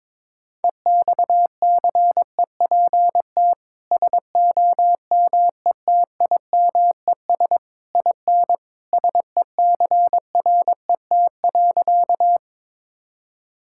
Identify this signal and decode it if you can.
Morse code, heard as a keyed tone, about 21 wpm, 700 Hz. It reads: EXCEPT SOMETIMES IN SECRET.
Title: Sentences from Top 1600 Words 22wpm